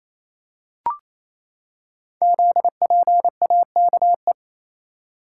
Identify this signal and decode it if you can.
Morse code, heard as a keyed tone, about 28 wpm, 700 Hz. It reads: ZPAKE
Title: Code Groups - 5 Letter:Numbers - 28wpm